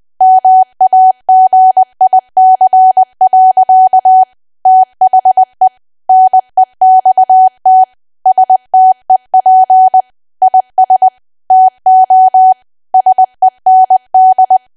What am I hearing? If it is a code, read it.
Morse code, heard as a keyed tone, about 20 wpm, 750 Hz. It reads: MAGIC. THE NEXT STEP IS TO SEND
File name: cwa / W1AW-Level3 / 20WPM